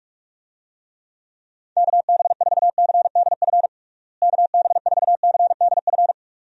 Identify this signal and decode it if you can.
Morse code, heard as a keyed tone, about 46 wpm, 700 Hz. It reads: KB4CDF KB4CDF
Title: USA Call Signs - 6 Characters - Repeated 2x - 45wpm